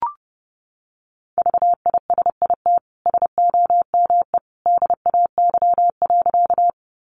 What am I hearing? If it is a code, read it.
Morse code, heard as a keyed tone, about 30 wpm, 700 Hz. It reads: VISIT SOME DAY.
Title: Sentences from Top 900 Words 30wpm